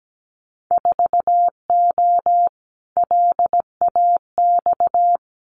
Morse code, heard as a keyed tone, about 17 wpm, 700 Hz. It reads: 4O LAX
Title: Winter Field Day Exchanges Encore 17wpm